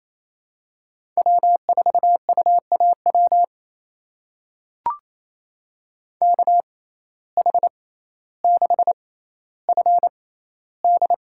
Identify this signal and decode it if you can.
Morse code, heard as a keyed tone, about 28 wpm, spaced as for 14 wpm, 700 Hz. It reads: W4UAW KH6FD
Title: USA Call Signs - 5 Characters - 6x Character Spacing - 28wpm